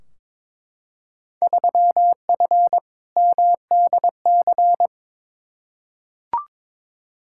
Morse code, heard as a keyed tone, about 22 wpm, 700 Hz. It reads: 3F MDC
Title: Field Day Exchanges Encore - 22wpm